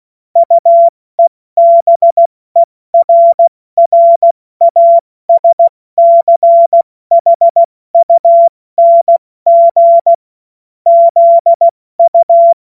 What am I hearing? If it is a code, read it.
Morse code, heard as a keyed tone, about 16 wpm, 680 Hz. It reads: UEBERRASCHUNG ZU
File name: Morsekurs / Witze / Witz 10. Januar 2023 70 F80